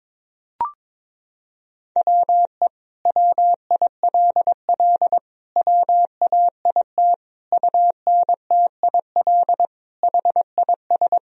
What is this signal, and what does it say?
Morse code, heard as a keyed tone, about 22 wpm, 700 Hz. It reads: WE WILL WAIT UNTIL HIS
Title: Sentences from Top 1000 Words - Review Part 1 - 22wpm